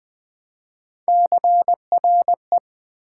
Morse code, heard as a keyed tone, about 20 wpm, 700 Hz. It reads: CRE
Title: Build A Word - 1 Letter At A Time - 4-Letter Words - 20wpm